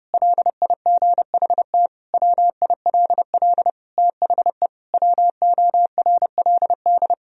Morse code, heard as a keyed tone, about 32 wpm, 700 Hz. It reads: LIGHT WILL THE WORLD
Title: Sentences from Top 1000 Words - Review Part 2 - 30wpm